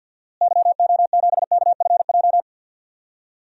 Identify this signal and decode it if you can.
Morse code, heard as a keyed tone, about 50 wpm, 700 Hz. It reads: YO7KFJ
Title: DXCC-100 Call Signs - 6 Characters - Repeated 2x - 50wpm